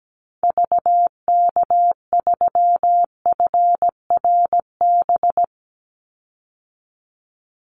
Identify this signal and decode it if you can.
Morse code, heard as a keyed tone, about 17 wpm, 700 Hz. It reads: VK3FRB
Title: USA Call Signs 17wpm